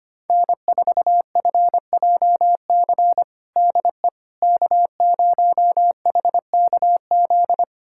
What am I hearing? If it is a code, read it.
Morse code, heard as a keyed tone, about 25 wpm, 700 Hz. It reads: N4FJC DE K0HKZ
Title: USA Callsign DE Callsign - 5 Characters - 25wpm